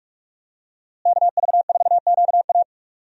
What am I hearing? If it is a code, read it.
Morse code, heard as a keyed tone, about 46 wpm, 700 Hz. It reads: KV4QA
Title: USA Call Signs - 5 Characters - Repeated 3x - 45wpm